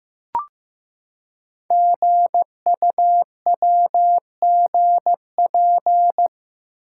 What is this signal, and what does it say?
Morse code, heard as a keyed tone, about 15 wpm, 700 Hz. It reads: GUWGP
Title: Code Groups - 5 Letters - 15wpm